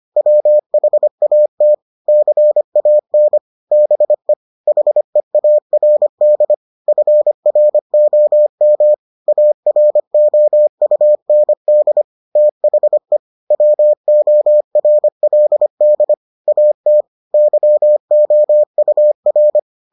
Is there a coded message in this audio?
Yes: Morse code from 0.2 to 19.6 s, heard as a keyed tone, about 25 wpm, 600 Hz. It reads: WHAT CAN BE HEARD FROM AROUND THE WORLD AT YOUR